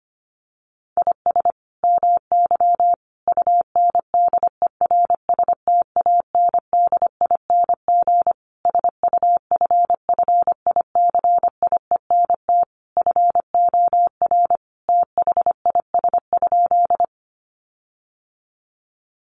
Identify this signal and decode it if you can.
Morse code, heard as a keyed tone, about 25 wpm, 700 Hz. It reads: IS MY UNDERSTANDING SUFFICIENT FOR THIS?
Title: Sentences from Top 1400 Words 25wpm